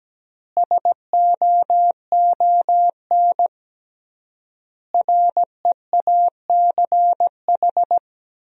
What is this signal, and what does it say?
Morse code, heard as a keyed tone, about 17 wpm, 700 Hz. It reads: SOON REACH
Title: 3x Word-Spacing - Sentences from Top 400 Words - 17wpm